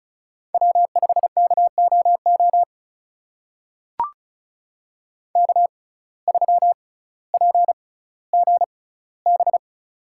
Morse code, heard as a keyed tone, about 36 wpm, spaced as for 18 wpm, 700 Hz. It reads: W5KOO K3PGB
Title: USA Call Signs - 5 Characters - 6x Character Spacing - 35wpm